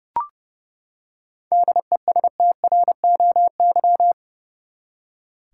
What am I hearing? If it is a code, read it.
Morse code, heard as a keyed tone, about 30 wpm, 700 Hz. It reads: DESTROY
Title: Seven Letter Words - 30wpm